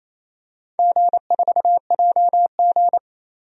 Morse code, heard as a keyed tone, about 28 wpm, 700 Hz. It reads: G4JG
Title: DXCC-100 Call Signs - 4 Characters - 28wpm